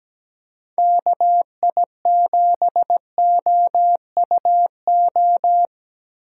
Morse code, heard as a keyed tone, about 17 wpm, 700 Hz. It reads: KI7OUO
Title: USA Call Signs - 2x Character Spacing - 17wpm